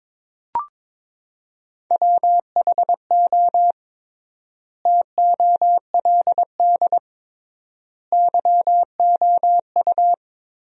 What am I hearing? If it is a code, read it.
Morse code, heard as a keyed tone, about 21 wpm, 700 Hz. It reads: WHO TOLD YOU
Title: 3x Word-Spacing - Sentences from Top 400 Words - 22wpm